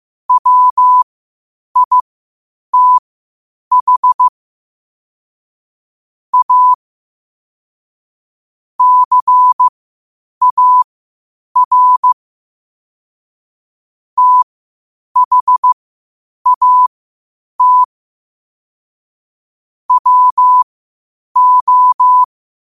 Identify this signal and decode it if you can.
Morse code, heard as a keyed tone, about 15 wpm, spaced as for 8 wpm, 1000 Hz. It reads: WITH A CAR THAT WO
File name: qotdfiles / 8wpm